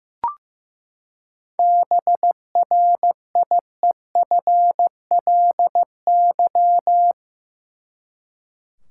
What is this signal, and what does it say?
Morse code, heard as a keyed tone, about 15 wpm, 700 Hz. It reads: BRIEFLY
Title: Seven Letter Words - 15wpm